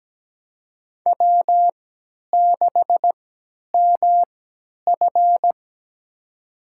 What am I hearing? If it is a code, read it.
Morse code, heard as a keyed tone, about 17 wpm, spaced as for 10 wpm, 700 Hz. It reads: W6MF
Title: USA Call Signs - 4 Characters - 3x Character Spacing - 17wpm